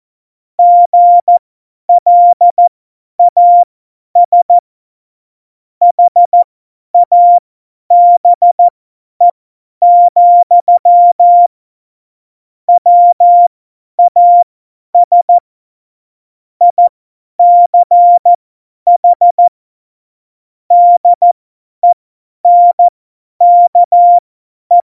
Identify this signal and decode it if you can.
Morse code, heard as a keyed tone, about 14 wpm, spaced as for 10 wpm, 700 Hz. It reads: GLAS HABE, WAS ICH DENKE
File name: Morsekurs / Witze / Witz 18. Februar 2020, 50 F70